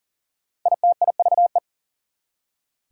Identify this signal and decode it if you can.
Morse code, heard as a keyed tone, about 40 wpm, 700 Hz. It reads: ITIVE
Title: Prefixes and Suffixes 40wpm